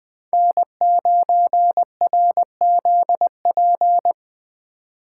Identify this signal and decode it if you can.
Morse code, heard as a keyed tone, about 21 wpm, 700 Hz. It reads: N9RZP